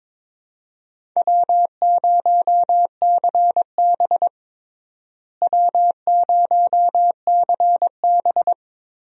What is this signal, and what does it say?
Morse code, heard as a keyed tone, about 22 wpm, 700 Hz. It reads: W0CB W0CB
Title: USA Call Signs - 4 Characters - Repeated 3x - 22wpm